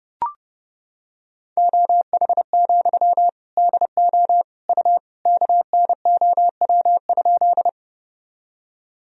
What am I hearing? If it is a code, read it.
Morse code, heard as a keyed tone, about 30 wpm, 700 Hz. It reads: OH, DO U KNOW?
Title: Sentences from Top 500 Words and Abbreviations 30wpm